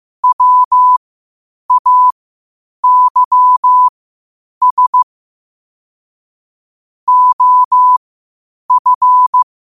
Morse code, heard as a keyed tone, about 15 wpm, spaced as for 8 wpm, 1000 Hz. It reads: WAYS OF
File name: qotdfiles / 8wpm